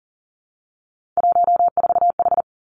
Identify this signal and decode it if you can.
Morse code, heard as a keyed tone, about 40 wpm, 700 Hz. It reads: 14H
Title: Three Challenging Letter-Number Combinations 40wpm